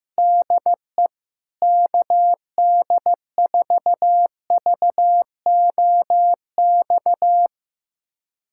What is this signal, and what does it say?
Morse code, heard as a keyed tone, about 15 wpm, 700 Hz. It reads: DE KD4VOX
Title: USA Callsign DE Callsign - 6 Characters - 15wpm